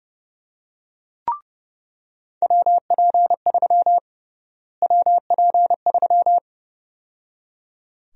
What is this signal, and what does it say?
Morse code, heard as a keyed tone, about 30 wpm, 700 Hz. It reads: WP3 WP3
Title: USA Call Sign Prefixes - Repeated 2x - 30wpm